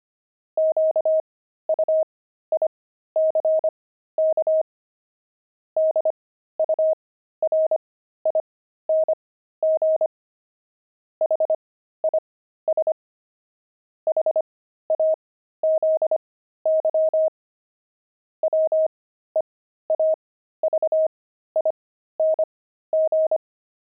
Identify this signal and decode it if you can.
Morse code, heard as a keyed tone, about 25 wpm, spaced as for 13 wpm, 630 Hz. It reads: QUICK DURING HIS HAZY WEAVING
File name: pangram-fs-13-cs-25-weaving